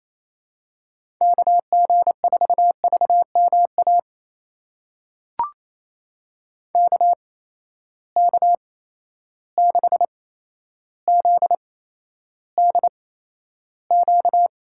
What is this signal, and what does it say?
Morse code, heard as a keyed tone, about 28 wpm, spaced as for 12 wpm, 700 Hz. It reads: KG4VMA KK6ZDQ
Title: USA Call Signs - 6 Characters - 8x Character Spacing - 28wpm